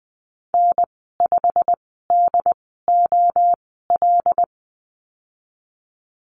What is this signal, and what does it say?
Morse code, heard as a keyed tone, about 20 wpm, spaced as for 15 wpm, 700 Hz. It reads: N5DOL
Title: 128 Lesson 21g - Number 6 - Call Signs 20wpm